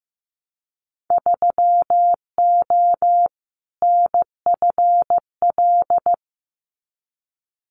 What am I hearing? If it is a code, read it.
Morse code, heard as a keyed tone, about 15 wpm, 700 Hz. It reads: 3O NFL
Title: Winter Field Day Exchanges 15wpm